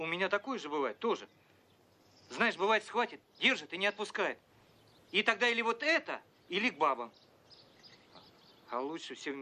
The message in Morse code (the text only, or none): none